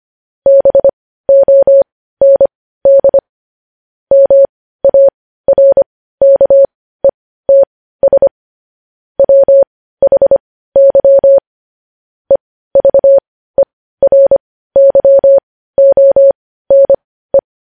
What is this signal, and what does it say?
Morse code, heard as a keyed tone, about 25 wpm, spaced as for 15 wpm, 550 Hz. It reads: BOND MARKETS WHY EVERYONE
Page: news-headlines-in-morse-code-at-15-wpm